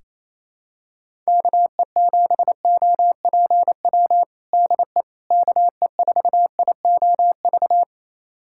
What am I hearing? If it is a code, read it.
Morse code, heard as a keyed tone, about 28 wpm, 700 Hz. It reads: KE7OPW DE KE4IOV
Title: USA Callsign DE Callsign - 6 Characters - 28wpm